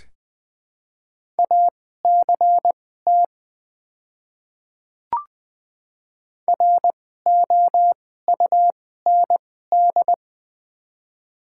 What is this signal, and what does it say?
Morse code, heard as a keyed tone, about 20 wpm, spaced as for 15 wpm, 690 Hz. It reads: ACT ROUND
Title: 2x Character Spacing - Top 300 Words - 20wpm